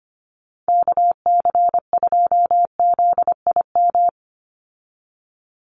Morse code, heard as a keyed tone, about 25 wpm, 700 Hz. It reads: KC2ZIM
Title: USA Call Signs 25wpm